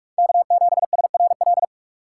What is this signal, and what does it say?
Morse code, heard as a keyed tone, about 44 wpm, 700 Hz. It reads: K7SRL
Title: USA Call Signs - 5 Characters - 3x Character Spacing - 45wpm